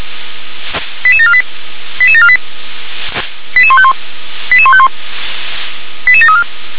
Five-tone selective call (ZVEI), heard as quick runs of five tones, tones of about 70 ms. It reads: 88748 88748 88151 88151 88734